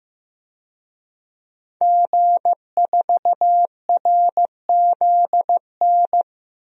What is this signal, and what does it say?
Morse code, heard as a keyed tone, about 15 wpm, 700 Hz. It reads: G4RZN
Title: DXCC-100 Call Signs - 5 Characters - 15wpm